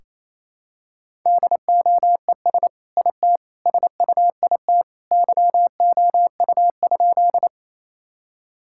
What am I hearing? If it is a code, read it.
Morse code, heard as a keyed tone, about 28 wpm, 700 Hz. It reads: DOES IT SUIT YOU?
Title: Sentences from Top 1600 Words 28wpm